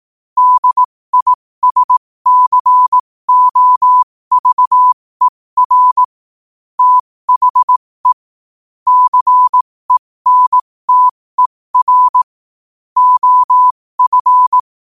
Morse code, heard as a keyed tone, about 18 wpm, spaced as for 15 wpm, 1000 Hz. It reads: DISCOVER THE CENTER OF